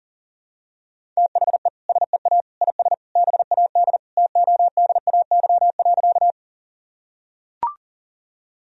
Morse code, heard as a keyed tone, about 40 wpm, 700 Hz. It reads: THE SEA IS BAD TODAY.
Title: Sentences from Top 1000 Words - Review Part 1 - 40wpm